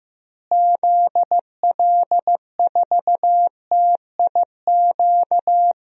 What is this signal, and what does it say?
Morse code, heard as a keyed tone, about 15 wpm, 700 Hz. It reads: ZL4TIQ
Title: DXCC-100 Call Signs - 6 Characters - Repeated 3x - 15wpm